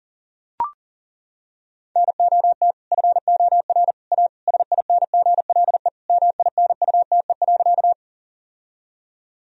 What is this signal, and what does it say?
Morse code, heard as a keyed tone, about 40 wpm, 700 Hz. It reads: NOT FOR A SINGLE MINUTE.
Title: Sentences from Top 1000 Words - Review Part 4 - 40wpm